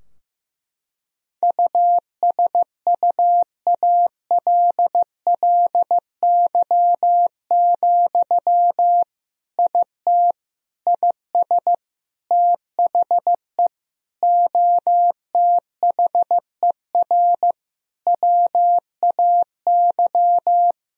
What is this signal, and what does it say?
Morse code, heard as a keyed tone, about 16 wpm, 700 Hz. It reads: USUALLY, IT IS THE OTHER WAY
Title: Sentences from Top 1000 Words - Review Part 3 - 15wpm